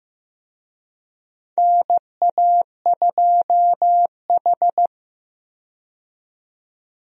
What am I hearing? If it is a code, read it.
Morse code, heard as a keyed tone, about 15 wpm, 700 Hz. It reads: NA2H
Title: USA Call Signs - 4 Characters - 15wpm